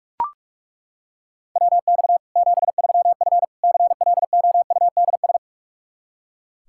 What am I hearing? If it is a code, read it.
Morse code, heard as a keyed tone, about 41 wpm, 700 Hz. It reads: WX 73F CLOUDS
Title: Practice QSO - Exchange 2 - Info - 45wpm